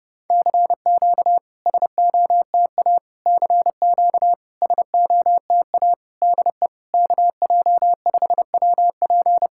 Morse code, heard as a keyed tone, about 30 wpm, 700 Hz. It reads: CQ SOTA CQ SOTA DE KJ5WP
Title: SOTA QSO Practice 30wpm